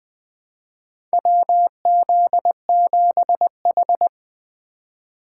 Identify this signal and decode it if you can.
Morse code, heard as a keyed tone, about 20 wpm, 700 Hz. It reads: WZ7H
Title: USA Call Signs - 2x Character Spacing - 20wpm